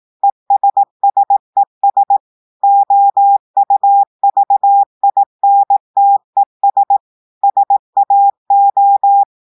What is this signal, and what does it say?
Morse code, heard as a keyed tone, about 18 wpm, 800 Hz. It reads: ESSES OUVINTES SAO
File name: audios / portugais / portugais texte en clair 2 vitesse 18 mots minutes